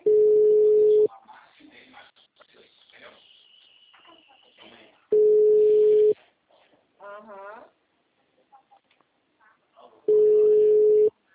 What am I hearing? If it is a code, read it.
Ringback tone (Europe, 425 Hz), with speech.
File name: out- -2002- - - .69571